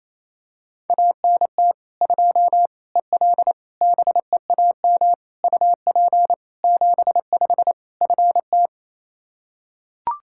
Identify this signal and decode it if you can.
Morse code, heard as a keyed tone, about 28 wpm, 700 Hz. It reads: ANT 2 EL BEAM UP 75 FT
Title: K1USN SST QSO Practice 28wpm